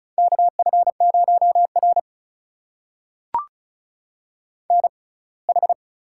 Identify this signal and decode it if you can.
Morse code, heard as a keyed tone, about 36 wpm, spaced as for 24 wpm, 700 Hz. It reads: KF0R NH
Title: USA Call Signs - 4 Characters - 6x Character Spacing - 35wpm